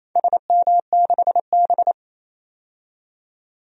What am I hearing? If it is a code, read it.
Morse code, heard as a keyed tone, about 28 wpm, 700 Hz. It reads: SM6B
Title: DXCC-100 Call Signs - 4 Characters - 28wpm